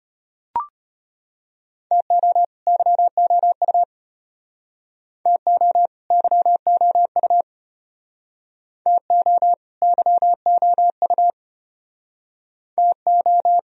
Morse code, heard as a keyed tone, about 34 wpm, 700 Hz. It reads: TO YOU TO YOU TO YOU TO